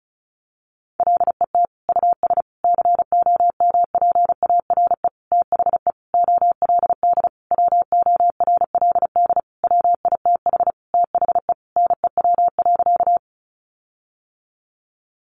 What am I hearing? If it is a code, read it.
Morse code, heard as a keyed tone, about 34 wpm, 700 Hz. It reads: LET US COMPARE THE OLD WORLD WITH THE NEW.
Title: Sentences from Top 1300 Words 35wpm